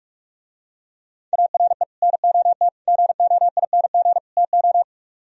Morse code, heard as a keyed tone, about 49 wpm, 700 Hz. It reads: ARE NOT GOING TO